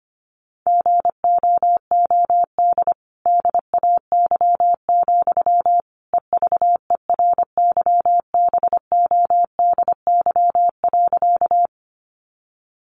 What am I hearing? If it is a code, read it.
Morse code, heard as a keyed tone, about 25 wpm, 700 Hz. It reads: GOOD DAY, EVERYBODY.
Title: Sentences from Top 1400 Words 25wpm